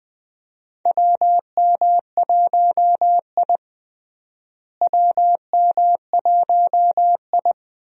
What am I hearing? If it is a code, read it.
Morse code, heard as a keyed tone, about 21 wpm, 700 Hz. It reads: WM1I WM1I
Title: USA Call Signs - 4 Characters - Repeated 3x - 20wpm